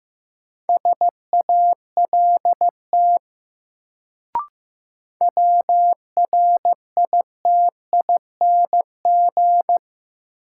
Morse code, heard as a keyed tone, about 15 wpm, 700 Hz. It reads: SALT WRITING
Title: Sentences from Top 800 Words - New Word Drill - 15wpm